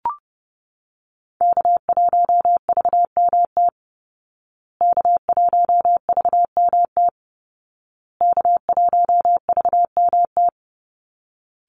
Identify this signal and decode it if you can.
Morse code, heard as a keyed tone, about 30 wpm, 700 Hz. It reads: K1VMT K1VMT K1VMT
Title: Call Signs Repeated 2x - 30wpm